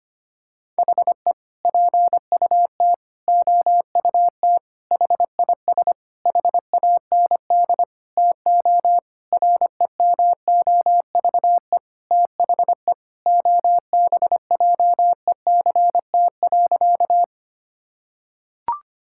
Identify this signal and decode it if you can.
Morse code, heard as a keyed tone, about 25 wpm, 700 Hz. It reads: HE PUT OUT HIS HAND TO REMOVE THE OBJECT.
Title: Sentences from Top 1000 Words - Review Part 5 - 25wpm